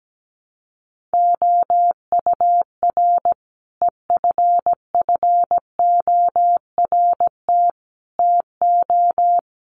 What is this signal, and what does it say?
Morse code, heard as a keyed tone, about 17 wpm, 700 Hz. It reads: OUR EFFORT TO
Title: Sentences from Top 1500 Words 17wpm